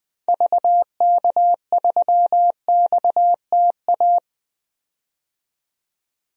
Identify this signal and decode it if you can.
Morse code, heard as a keyed tone, about 20 wpm, 700 Hz. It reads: VK3XTA
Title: DXCC-100 Call Signs - 6 Characters - 20wpm